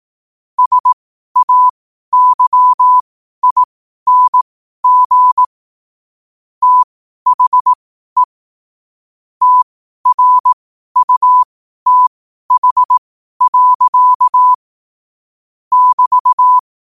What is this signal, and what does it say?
Morse code, heard as a keyed tone, about 18 wpm, spaced as for 12 wpm, 1000 Hz. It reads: SAYING THE TRUTH. <BT>
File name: qotdfiles / 12wpm